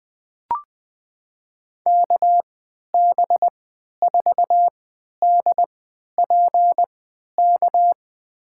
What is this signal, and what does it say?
Morse code, heard as a keyed tone, about 20 wpm, spaced as for 11 wpm, 700 Hz. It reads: KB4DPK
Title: USA Call Signs - 6 Characters - 3x Character Spacing - 20wpm